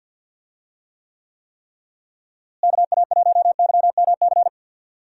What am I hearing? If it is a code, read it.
Morse code, heard as a keyed tone, about 49 wpm, 700 Hz. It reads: KA1YMC
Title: USA Call Signs - 6 Characters - Repeated 3x - 50wpm